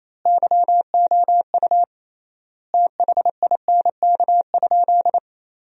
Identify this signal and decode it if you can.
Morse code, heard as a keyed tone, about 27 wpm, 700 Hz. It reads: YOU THINK?
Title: 3x Word-Spacing - Sentences from Top 200 Words - 28wpm